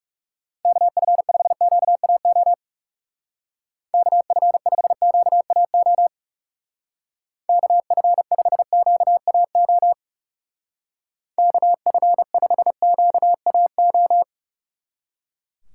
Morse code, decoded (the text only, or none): KF5QAO KF5QAO KF5QAO KF5QAO